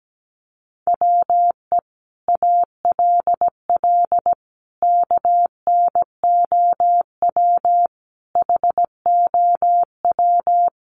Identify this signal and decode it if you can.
Morse code, heard as a keyed tone, about 17 wpm, 700 Hz. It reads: WE ALL KNOW HOW
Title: Sentences from Top 1200 Words 17wpm